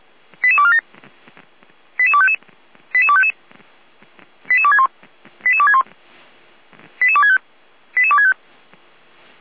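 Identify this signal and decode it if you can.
Five-tone selective call (ZVEI), heard as quick runs of five tones, tones of about 70 ms. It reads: 88247 88255 88255 88261 88261 88265 88265
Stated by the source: ZVEI five-tone selective call 88231, received via SDR.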